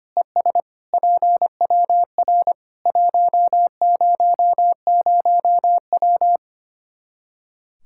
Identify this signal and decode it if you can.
Morse code, heard as a keyed tone, about 25 wpm, 700 Hz. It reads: ES PWR 100W
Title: Practice QSO - Exchange 2 - Info - 25wpm